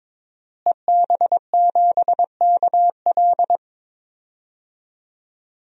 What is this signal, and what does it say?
Morse code, heard as a keyed tone, about 22 wpm, 700 Hz. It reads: EB7KL
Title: DXCC-100 Call Signs - 5 Characters - Repeated 2x - 22wpm